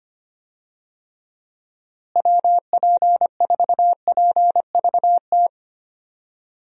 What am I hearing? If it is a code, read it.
Morse code, heard as a keyed tone, about 25 wpm, 700 Hz. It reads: WP4PVT
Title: USA Call Signs - 6 Characters - Repeated 2x - 25wpm